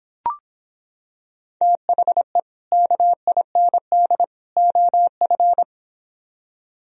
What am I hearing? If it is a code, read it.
Morse code, heard as a keyed tone, about 26 wpm, 700 Hz. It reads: THE KIND OF